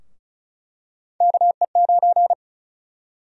Morse code, heard as a keyed tone, about 34 wpm, 700 Hz. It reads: KE9
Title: USA Call Sign Prefixes - 35wpm